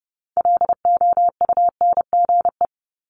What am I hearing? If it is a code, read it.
Morse code, heard as a keyed tone, about 31 wpm, 700 Hz. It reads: LOUNGE
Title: Multi-letter Phonemes 30wpm